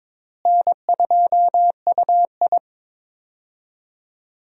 Morse code, heard as a keyed tone, about 22 wpm, 700 Hz. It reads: N2UI
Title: USA Call Signs - 4 Characters - Repeated 3x - 22wpm